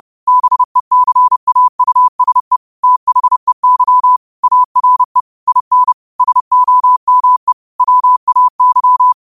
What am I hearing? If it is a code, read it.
Morse code, heard as a keyed tone, about 30 wpm, 1000 Hz. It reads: BECAUSE THEY ARE IN SOME WAY
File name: qotdfiles / 30wpm